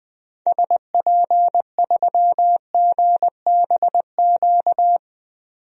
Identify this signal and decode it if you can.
Morse code, heard as a keyed tone, about 20 wpm, 700 Hz. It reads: SP3GBQ
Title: DXCC-100 Call Signs - 6 Characters - 20wpm